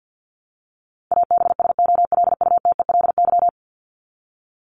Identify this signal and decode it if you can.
Morse code, heard as a keyed tone, about 50 wpm, 700 Hz. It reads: ABSOLUTELY
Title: LY Suffix - Top 100 Words with Base Words 50wpm